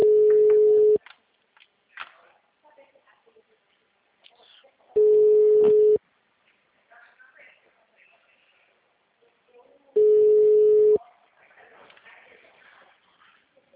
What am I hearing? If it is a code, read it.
Ringback tone (Europe, 425 Hz), with speech.
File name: out- -2033- - - .6008